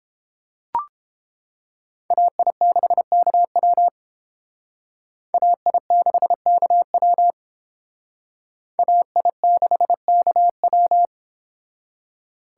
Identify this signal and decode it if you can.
Morse code, heard as a keyed tone, about 29 wpm, 700 Hz. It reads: AI6KW AI6KW AI6KW